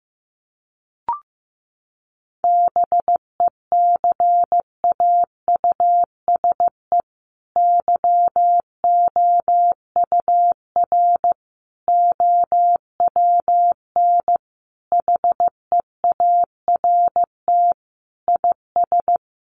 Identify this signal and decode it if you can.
Morse code, heard as a keyed tone, about 15 wpm, 700 Hz. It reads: BECAUSE YOUR OWN HEART IS
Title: Sentences from Top 1200 Words 15wpm